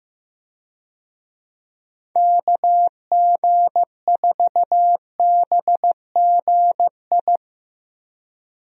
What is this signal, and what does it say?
Morse code, heard as a keyed tone, about 15 wpm, 700 Hz. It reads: KG4BGI
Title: USA Call Signs - 6 Characters - Repeated 2x - 15wpm